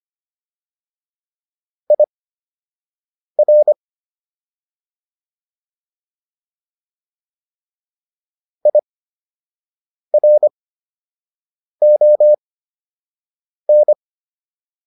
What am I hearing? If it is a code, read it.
Morse code, heard as a keyed tone, about 25 wpm, spaced as for 6 wpm, 600 Hz. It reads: IR IRON